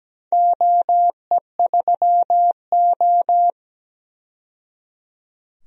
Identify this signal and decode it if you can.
Morse code, heard as a keyed tone, about 17 wpm, 700 Hz. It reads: OE3O
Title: DXCC-100 Call Signs - 4 Characters - 17wpm